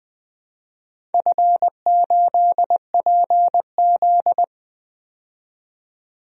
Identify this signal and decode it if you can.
Morse code, heard as a keyed tone, about 20 wpm, 700 Hz. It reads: F8PZ
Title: DXCC-100 Call Signs - 4 Characters - 20wpm